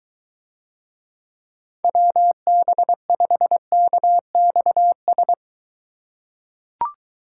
Morse code, heard as a keyed tone, about 22 wpm, 700 Hz. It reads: WB5KXS